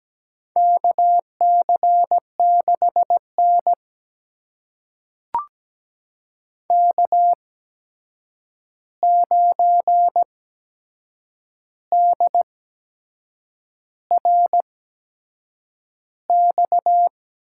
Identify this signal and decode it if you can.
Morse code, heard as a keyed tone, about 17 wpm, spaced as for 7 wpm, 700 Hz. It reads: KC6N K9DRX
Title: USA Call Signs - 8x Character Spacing - 17wpm